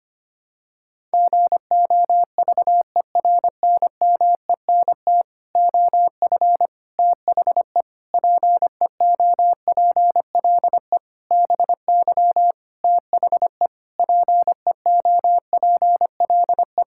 Morse code, heard as a keyed tone, about 25 wpm, 700 Hz. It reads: GOVERNMENT OF THE PEOPLE BY THE PEOPLE
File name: Sets of 9 Words - 25wpm